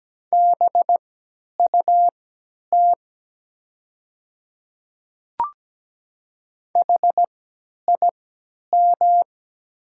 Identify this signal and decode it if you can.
Morse code, heard as a keyed tone, about 17 wpm, spaced as for 10 wpm, 700 Hz. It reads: BUT HIM
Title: 3x Character Spacing - Top 100 Words - 17wpm